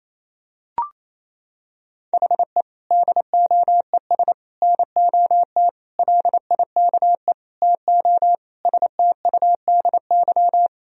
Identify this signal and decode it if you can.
Morse code, heard as a keyed tone, about 28 wpm, 700 Hz. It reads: HE DOES NOT LIKE TO STUDY
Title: Sentences from Top 1000 Words - Review Part 3 - 28wpm